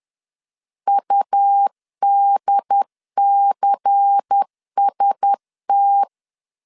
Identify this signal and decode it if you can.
Morse code, heard as a keyed tone, about 10 wpm, 800 Hz. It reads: UDCST